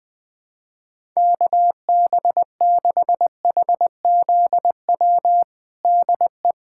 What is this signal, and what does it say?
Morse code, heard as a keyed tone, about 20 wpm, 700 Hz. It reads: KB6HZW DE
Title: USA Callsign DE Callsign - 6 Characters - 20wpm